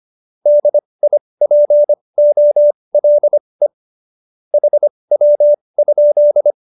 Morse code, heard as a keyed tone, about 25 wpm, spaced as for 20 wpm, 570 Hz. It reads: DIPOLE HW?
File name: QSO317 20